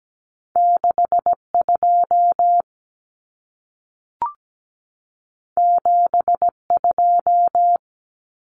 Morse code, heard as a keyed tone, about 17 wpm, 700 Hz. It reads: 62 72
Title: Numbers - 2-Digit 17wpm